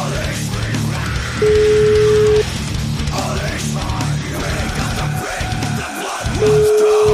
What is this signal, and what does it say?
Ringback tone (Europe, 425 Hz), with noise.